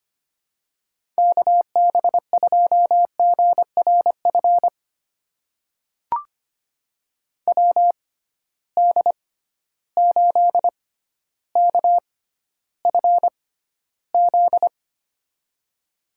Morse code, heard as a keyed tone, about 25 wpm, spaced as for 13 wpm, 700 Hz. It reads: KB2GRF WD8KFZ
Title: USA Call Signs - 6 Characters - 6x Character Spacing - 25wpm